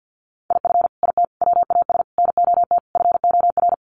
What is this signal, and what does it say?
Morse code, heard as a keyed tone, about 50 wpm, 700 Hz. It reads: IF IT WAS NOT FOR
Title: Sets of 5 Words Encore 50wpm